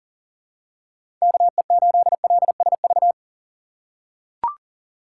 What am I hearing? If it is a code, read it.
Morse code, heard as a keyed tone, about 40 wpm, 700 Hz. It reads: KE8LSV
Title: USA Call Signs - 4x Character Spacing - 40wpm